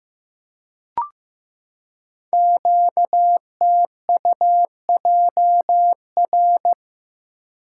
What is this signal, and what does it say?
Morse code, heard as a keyed tone, about 15 wpm, 700 Hz. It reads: QTUJR
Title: Code Groups - 5 Letters - 15wpm